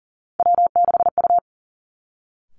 Morse code, heard as a keyed tone, about 40 wpm, 690 Hz. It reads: W6U
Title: Three Challenging Letter-Number Combinations 40wpm